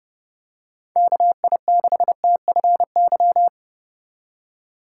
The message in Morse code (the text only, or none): KI6TFY